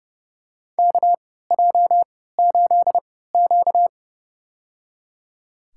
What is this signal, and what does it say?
Morse code, heard as a keyed tone, about 30 wpm, spaced as for 17 wpm, 700 Hz. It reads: KJ8Q
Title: USA Call Signs - 3x Character Spacing - 30wpm